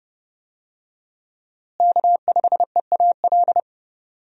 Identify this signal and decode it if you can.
Morse code, heard as a keyed tone, about 30 wpm, 700 Hz. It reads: K5EAL
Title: USA Call Signs - 5 Characters - Repeated 3x - 30wpm